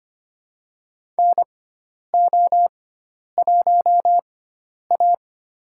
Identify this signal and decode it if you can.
Morse code, heard as a keyed tone, about 25 wpm, spaced as for 10 wpm, 700 Hz. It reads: NO1A
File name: Lesson 16g - Letter c - Call Signs 25wpm